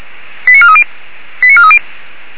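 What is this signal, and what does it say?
Five-tone selective call (ZVEI), heard as quick runs of five tones, tones of about 70 ms. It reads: 89430 89430
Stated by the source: ZVEI five-tone selective call 89315 received on an SDR, from the Feuerwehr Orlamünde fire brigade.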